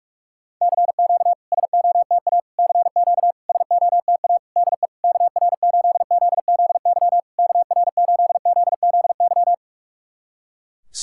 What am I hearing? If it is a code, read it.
Morse code, heard as a keyed tone, about 44 wpm, 700 Hz. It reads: CQ SOTA CQ SOTA DE KR8ZZY KR8ZZY
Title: SOTA QSO Practice 45wpm